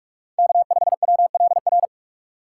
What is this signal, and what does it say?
Morse code, heard as a keyed tone, about 46 wpm, 700 Hz. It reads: K5WLR
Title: USA Call Signs - 5 Characters - 3x Character Spacing - 45wpm